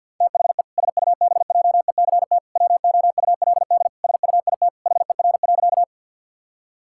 Morse code, heard as a keyed tone, about 50 wpm, 700 Hz. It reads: THE SUBJECT WOULD SUIT HER.
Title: Sentences from Top 1600 Words 50wpm